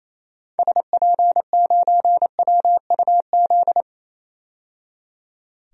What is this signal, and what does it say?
Morse code, heard as a keyed tone, about 28 wpm, 700 Hz. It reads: SP9WUZ
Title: DXCC-100 Call Signs - 6 Characters - 28wpm